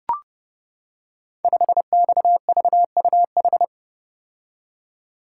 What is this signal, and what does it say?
Morse code, heard as a keyed tone, about 28 wpm, 700 Hz. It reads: 5XVUH
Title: Code Groups - 5 Letter:Numbers - 30wpm